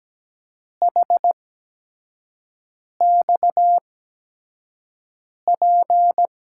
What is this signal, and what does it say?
Morse code, heard as a keyed tone, about 17 wpm, spaced as for 5 wpm, 700 Hz. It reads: HXP
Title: USA Call Signs - 6 Characters - 8x Character Spacing - 17wpm